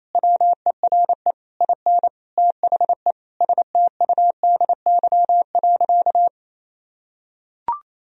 Morse code, heard as a keyed tone, about 28 wpm, 700 Hz. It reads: WERE IN THE STUDY.
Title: Sentences from Top 1000 Words - Review Part 1 - 28wpm